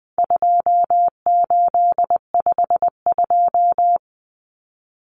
Morse code, heard as a keyed tone, about 20 wpm, 700 Hz. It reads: 2852
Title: Common Cut Numbers - 4-Digit 20wpm